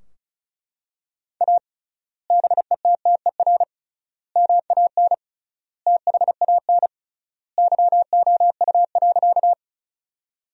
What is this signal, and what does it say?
Morse code, heard as a keyed tone, about 36 wpm, 700 Hz. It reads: A BETTER MAN THAN YOU.
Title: 3x Word-Spacing - Sentences from Top 400 Words - 35wpm